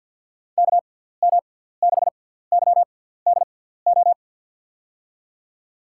Morse code, heard as a keyed tone, about 49 wpm, spaced as for 17 wpm, 700 Hz. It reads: KM6YDO
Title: USA Call Signs - 6x Character Spacing - 50wpm